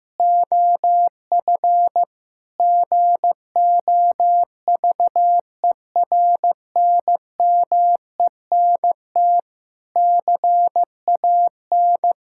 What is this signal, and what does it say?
Morse code, heard as a keyed tone, about 15 wpm, 700 Hz. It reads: OF GOVERNMENT CAN
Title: Sentences from Top 1000 Words - Review Part 6 - 15wpm